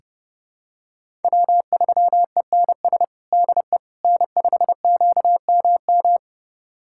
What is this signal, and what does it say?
Morse code, heard as a keyed tone, about 32 wpm, 700 Hz. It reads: W3ENS DE N5QMM
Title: USA Callsign DE Callsign - 5 Characters - 30wpm